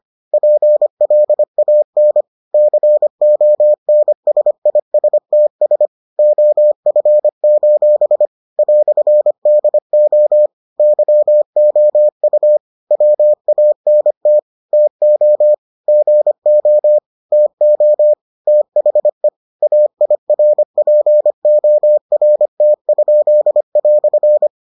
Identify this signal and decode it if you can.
Morse code, heard as a keyed tone, about 25 wpm, 600 Hz. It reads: PLAN CONSISTS OF: "DO YOU WANT TO GO TO THE AIRPORT?"